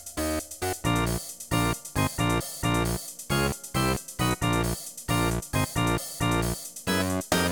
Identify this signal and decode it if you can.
There is no code.